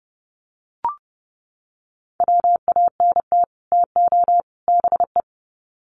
Morse code, heard as a keyed tone, about 29 wpm, 700 Hz. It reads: WANT TO BE
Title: Sets of 3 Words 30wpm